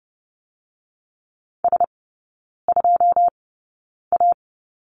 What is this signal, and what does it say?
Morse code, heard as a keyed tone, about 30 wpm, spaced as for 9 wpm, 700 Hz. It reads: S2A
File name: 081 Lesson 14e - Number 2 - Three Characters 30wpm